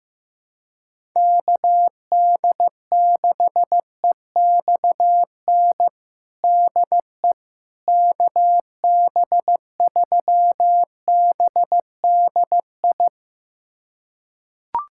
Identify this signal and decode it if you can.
Morse code, heard as a keyed tone, about 15 wpm, 700 Hz. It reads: KD6EXN DE KB3BDI
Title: USA Callsign DE Callsign 15wpm